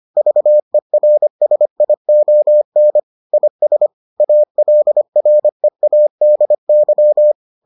Morse code, heard as a keyed tone, about 25 wpm, 600 Hz. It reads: VERSION IS ALREADY